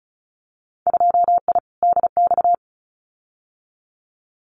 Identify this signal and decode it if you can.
Morse code, heard as a keyed tone, about 34 wpm, 700 Hz. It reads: 2I DX
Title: Winter Field Day Exchanges 35wpm